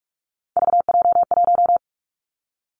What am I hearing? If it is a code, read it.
Morse code, heard as a keyed tone, about 46 wpm, 700 Hz. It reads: VJ1
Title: Three Challenging Letter-Number Combinations 45wpm